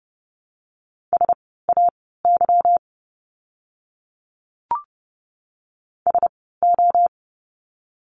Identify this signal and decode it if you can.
Morse code, heard as a keyed tone, about 30 wpm, spaced as for 17 wpm, 700 Hz. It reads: SAY SO
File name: course / 30:10wpm / 230 Lesson 37f - Letter x - Words 30wpm